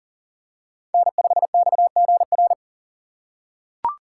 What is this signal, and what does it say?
Morse code, heard as a keyed tone, about 40 wpm, 700 Hz. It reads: N5XGR
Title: USA Call Signs - 5 Characters - 2x Character Spacing - 40wpm